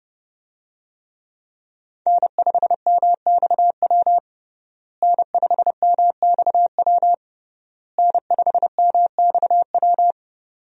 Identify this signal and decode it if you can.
Morse code, heard as a keyed tone, about 32 wpm, 700 Hz. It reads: N5MXW N5MXW N5MXW
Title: USA Call Signs - 5 Characters - Repeated 3x - 30wpm